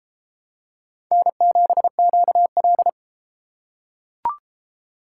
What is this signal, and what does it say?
Morse code, heard as a keyed tone, about 31 wpm, 700 Hz. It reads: N7QL